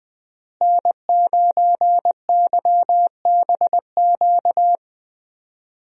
Morse code, heard as a keyed tone, about 21 wpm, 700 Hz. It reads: N9YBQ
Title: USA Call Signs - 5 Characters - Repeated 3x - 20wpm